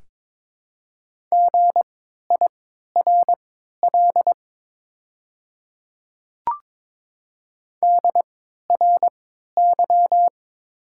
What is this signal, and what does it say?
Morse code, heard as a keyed tone, about 22 wpm, spaced as for 12 wpm, 690 Hz. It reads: GIRL DRY
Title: 3x Character Spacing - Top 500 Words - 22wpm